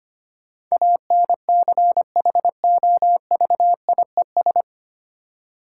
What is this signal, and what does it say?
Morse code, heard as a keyed tone, about 25 wpm, 700 Hz. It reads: ANCHOVIES
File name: practice / 25wpm / Difficult Words with Hints - 25wpm